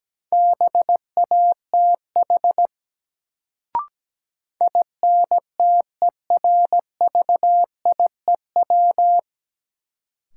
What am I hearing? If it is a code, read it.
Morse code, heard as a keyed tone, about 17 wpm, 700 Hz. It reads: BATH INTERVIEW
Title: Sentences from Top 1600 Words - New Word Drill Alt - 17wpm